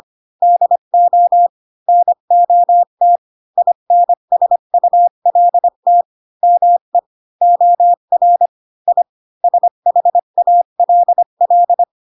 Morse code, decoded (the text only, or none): DO NOT INSULT ME OR I SHALL